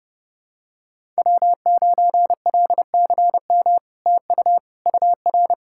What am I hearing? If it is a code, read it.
Morse code, heard as a keyed tone, about 30 wpm, 700 Hz. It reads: W9LCM TU UR
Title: SOTA QSO Practice 30wpm